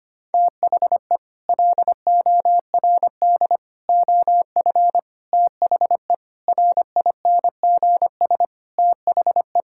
Morse code, heard as a keyed tone, about 25 wpm, 700 Hz. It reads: THE LORD OF THE RINGS THE
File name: Sets of 9 Words - 25wpm